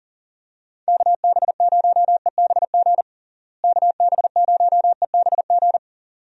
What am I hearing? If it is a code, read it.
Morse code, heard as a keyed tone, about 37 wpm, 700 Hz. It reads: KB0EBG KB0EBG
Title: USA Call Signs - 6 Characters - Repeated 2x - 40wpm